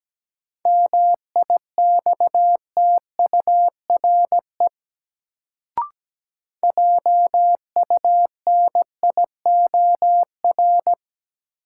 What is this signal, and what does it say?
Morse code, heard as a keyed tone, about 17 wpm, 700 Hz. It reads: MIXTURE JUNIOR
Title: Sentences from Top 1500 Words - New Word Drill Alt - 17wpm